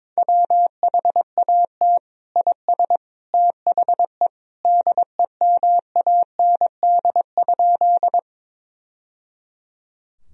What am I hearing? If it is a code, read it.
Morse code, heard as a keyed tone, about 22 wpm, 700 Hz. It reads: WHAT IS THE DEMAND?
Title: Sentences from Top 1000 Words - Review Part 4 - 22wpm